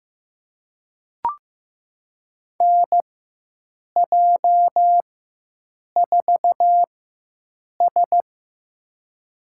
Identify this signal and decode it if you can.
Morse code, heard as a keyed tone, about 15 wpm, spaced as for 7 wpm, 700 Hz. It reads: NJ4S
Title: USA Call Signs - 4 Characters - 4x Character Spacing - 15wpm